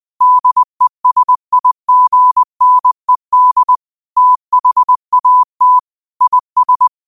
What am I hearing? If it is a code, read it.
Morse code, heard as a keyed tone, about 20 wpm, 1000 Hz. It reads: DESIGNED THAT IS